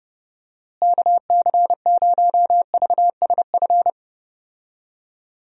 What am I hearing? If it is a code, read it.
Morse code, heard as a keyed tone, about 32 wpm, 700 Hz. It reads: KC0VSF
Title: USA Call Signs - 6 Characters - Repeated 2x - 30wpm